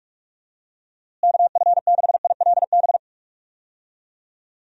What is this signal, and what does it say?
Morse code, heard as a keyed tone, about 41 wpm, 700 Hz. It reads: KF6ILB
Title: USA Call Signs - 6 Characters - Repeated 2x - 45wpm